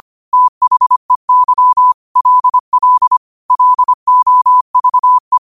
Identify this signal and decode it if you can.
Morse code, heard as a keyed tone, about 25 wpm, 1000 Hz. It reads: THEY LL LOVE
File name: qotdfiles / 25wpm